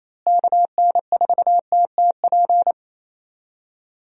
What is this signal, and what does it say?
Morse code, heard as a keyed tone, about 28 wpm, 700 Hz. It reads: KN4TTP
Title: USA Call Signs - 6 Characters - Repeated 2x - 28wpm